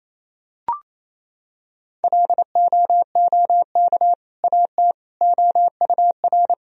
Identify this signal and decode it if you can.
Morse code, heard as a keyed tone, about 28 wpm, 700 Hz. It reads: LOOK AT OUR
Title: Sentences from Top 1000 Words - Review Part 3 - 28wpm